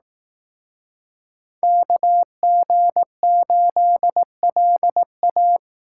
Morse code, heard as a keyed tone, about 18 wpm, 700 Hz. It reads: KG8LA
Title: Speed Racing - USA Call Signs 15wpm